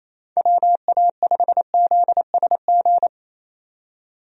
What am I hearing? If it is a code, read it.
Morse code, heard as a keyed tone, about 28 wpm, 700 Hz. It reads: WA5ZSG
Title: USA Call Signs - 6 Characters - 28wpm